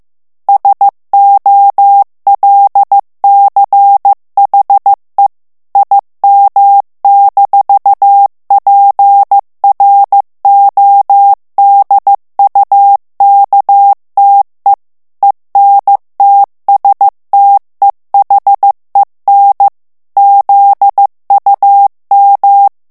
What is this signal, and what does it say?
Morse code, heard as a keyed tone, about 15 wpm, 800 Hz. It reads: SOLCHE IM-PRODUKTE ENTSTEHEN ZUM
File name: morse mp3 / ALLEMAND texte clair 1 vitesse 14 mots minute